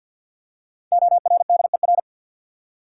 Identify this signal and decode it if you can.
Morse code, heard as a keyed tone, about 53 wpm, 700 Hz. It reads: ORDER
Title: Five Letter Words - 50wpm